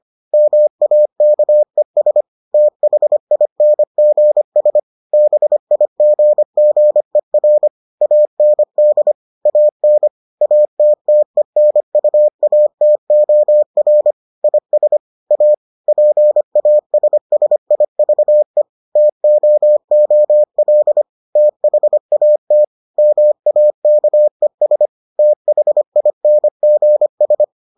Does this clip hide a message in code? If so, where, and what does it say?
Yes: Morse code from 0.3 to 27.4 s, heard as a keyed tone, about 25 wpm, 600 Hz. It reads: MAKES THINGS BIGGER AND AN ATTENUATOR IS A PASSIVE TOOL THAT MAKES THINGS